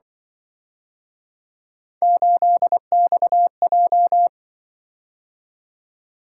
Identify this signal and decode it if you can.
Morse code, heard as a keyed tone, about 24 wpm, 700 Hz. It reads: 8XJ
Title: Speed-Racing - Three Letter-Number Combinations - 20wpm